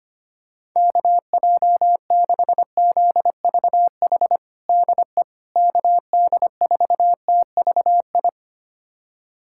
Morse code, heard as a keyed tone, about 25 wpm, 700 Hz. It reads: KJ6ZVH DE KD4TVI
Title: USA Callsign DE Callsign 25wpm